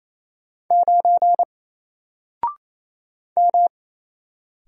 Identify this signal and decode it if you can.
Morse code, heard as a keyed tone, about 28 wpm, 700 Hz. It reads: 9 M
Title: Single Character (Letter:Number:Punctuation:Prosign) 28wpm